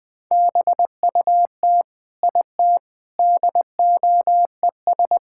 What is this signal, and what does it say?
Morse code, heard as a keyed tone, about 20 wpm, 700 Hz. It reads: BUT IT DOES
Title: Sentences with 8-Word Phrases - 20wpm